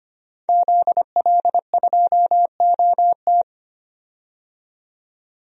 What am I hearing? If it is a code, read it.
Morse code, heard as a keyed tone, about 25 wpm, 700 Hz. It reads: ZL2OT
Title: DXCC-100 Call Signs - 5 Characters - 25wpm